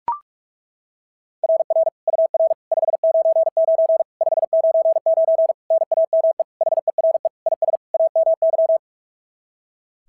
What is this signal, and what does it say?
Morse code, heard as a keyed tone, about 42 wpm, 660 Hz. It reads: RR UR 599 599 NAME HERE IS AMY
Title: SOTA QSO Practice 45wpm